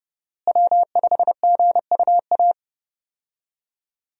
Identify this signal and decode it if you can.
Morse code, heard as a keyed tone, about 30 wpm, 700 Hz. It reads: W5GUA
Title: USA Call Signs - 5 Characters - Repeated 2x - 30wpm